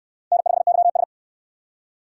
Morse code, heard as a keyed tone, about 68 wpm, 700 Hz. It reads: N5YS